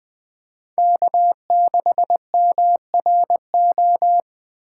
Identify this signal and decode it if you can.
Morse code, heard as a keyed tone, about 20 wpm, 700 Hz. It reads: K6MRO
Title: USA Call Signs - 5 Characters - 2x Character Spacing - 20wpm